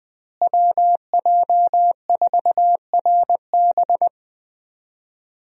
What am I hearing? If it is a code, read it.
Morse code, heard as a keyed tone, about 21 wpm, 700 Hz. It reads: WJ4RB
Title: USA Call Signs - 5 Characters - Repeated 3x - 20wpm